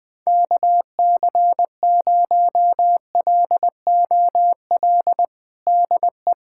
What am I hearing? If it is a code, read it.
Morse code, heard as a keyed tone, about 20 wpm, 700 Hz. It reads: KC0LOL DE
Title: USA Callsign DE Callsign 20wpm